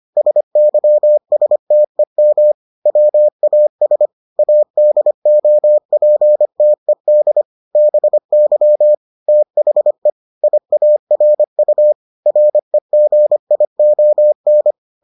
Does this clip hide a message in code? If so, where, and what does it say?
Yes: Morse code from 0.2 to 14.7 s, heard as a keyed tone, about 25 wpm, 600 Hz. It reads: SYSTEM WAS ADOPTED BY THE IARU REGION